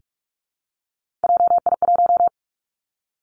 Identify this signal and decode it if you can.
Morse code, heard as a keyed tone, about 46 wpm, 700 Hz. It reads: JI1
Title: Three Challenging Letter-Number Combinations 45wpm